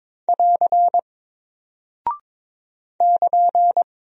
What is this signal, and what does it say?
Morse code, heard as a keyed tone, about 22 wpm, 700 Hz. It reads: <AR> <KN>
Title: Single Character (Letter:Number:Punctuation:Prosign) Alt 22wpm